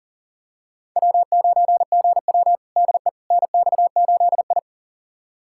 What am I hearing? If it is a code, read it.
Morse code, heard as a keyed tone, about 40 wpm, 700 Hz. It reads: W9GW DE NX8I
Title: USA Callsign DE Callsign - 4 Characters - 40wpm